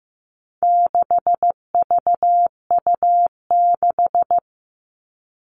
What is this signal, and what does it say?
Morse code, heard as a keyed tone, about 15 wpm, 700 Hz. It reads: 6VU6
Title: All Cut Numbers - 4-Digit 15wpm